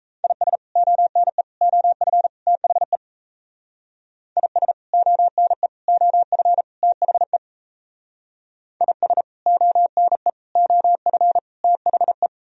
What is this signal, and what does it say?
Morse code, heard as a keyed tone, about 35 wpm, 700 Hz. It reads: IS ONE OF THE IS ONE OF THE IS ONE OF THE